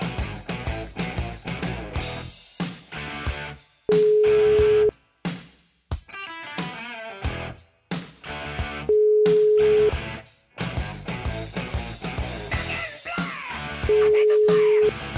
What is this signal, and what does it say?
Ringback tone (Europe, 425 Hz).